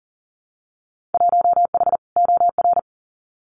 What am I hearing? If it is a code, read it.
Morse code, heard as a keyed tone, about 41 wpm, 700 Hz. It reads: 1H OR
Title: Winter Field Day Exchanges 40wpm